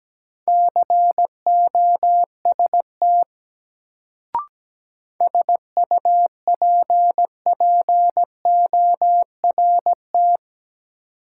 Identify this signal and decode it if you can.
Morse code, heard as a keyed tone, about 17 wpm, 700 Hz. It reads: COST SUPPORT
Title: Sentences from Top 800 Words - New Word Drill - 17wpm